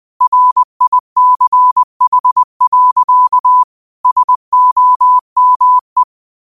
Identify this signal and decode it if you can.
Morse code, heard as a keyed tone, about 20 wpm, 1000 Hz. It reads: RICH. SOME
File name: 20wpm